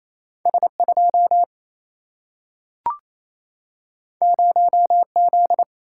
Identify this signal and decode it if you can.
Morse code, heard as a keyed tone, about 28 wpm, 700 Hz. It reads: S 2 0 Z
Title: Single-Letter Single-Number 28wpm